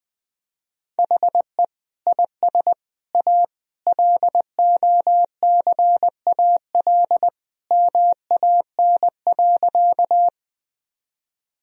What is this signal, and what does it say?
Morse code, heard as a keyed tone, about 20 wpm, 700 Hz. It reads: HE IS A LOCAL MAN.
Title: Sentences from Top 1000 Words - Review Part 3 - 20wpm